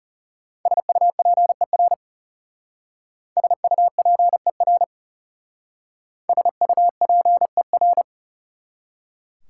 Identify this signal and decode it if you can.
Morse code, heard as a keyed tone, about 38 wpm, 700 Hz. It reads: SUPER SUPER SUPER